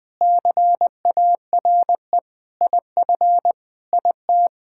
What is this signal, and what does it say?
Morse code, heard as a keyed tone, about 20 wpm, 700 Hz. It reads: CARE IF IT
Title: Sentences with 5-Word Phrases - 20wpm